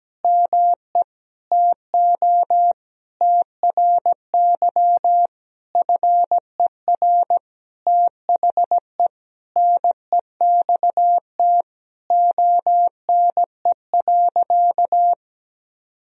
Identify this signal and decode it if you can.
Morse code, heard as a keyed tone, about 17 wpm, 700 Hz. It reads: ME TO TRY FER THE NEXT ONE.
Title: Sentences from Top 400 Words and Abbreviations 17wpm